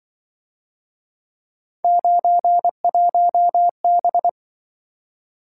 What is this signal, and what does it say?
Morse code, heard as a keyed tone, about 24 wpm, 700 Hz. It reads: 91B
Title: Speed-Racing - Three Letter-Number Combinations - 20wpm